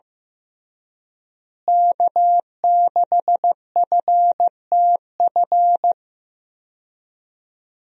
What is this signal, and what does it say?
Morse code, heard as a keyed tone, about 15 wpm, 700 Hz. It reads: K6FTF
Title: USA Call Signs - 5 Characters - Repeated 3x - 15wpm